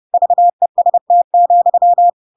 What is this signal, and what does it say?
Morse code, heard as a keyed tone, about 30 wpm, 700 Hz. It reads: VEST,
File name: Pinocchio / Pinocchio ch8 30 30 ik1hge